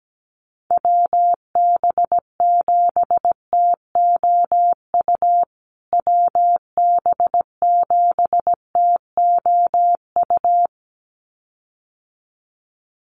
Morse code, heard as a keyed tone, about 17 wpm, 700 Hz. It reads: WB7TOU WB7TOU
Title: USA Call Signs Repeated 2x 17wpm